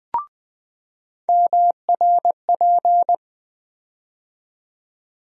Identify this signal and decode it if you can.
Morse code, heard as a keyed tone, about 20 wpm, 700 Hz. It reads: MRP
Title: USA Call Sign Postfixes - 3 Letters 20wpm